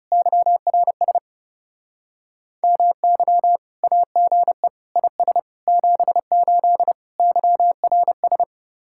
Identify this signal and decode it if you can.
Morse code, heard as a keyed tone, about 39 wpm, 700 Hz. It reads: YRS MY AGE IS 78 YRS